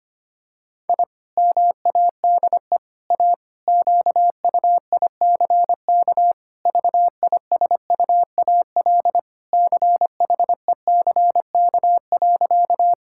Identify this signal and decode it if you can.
Morse code, heard as a keyed tone, about 25 wpm, 700 Hz. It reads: I MADE A QUICK VISUAL CHECK.
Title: Sentences from Top 1000 Words - Review Part 6 - 25wpm